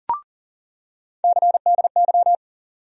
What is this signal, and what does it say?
Morse code, heard as a keyed tone, about 40 wpm, 700 Hz. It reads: CDY
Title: USA Call Sign Postfixes - 3 Letters 40wpm